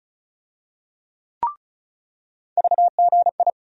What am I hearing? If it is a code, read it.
Morse code, heard as a keyed tone, about 36 wpm, 700 Hz. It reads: VGI
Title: USA Call Sign Postfixes - 3 Letters 35wpm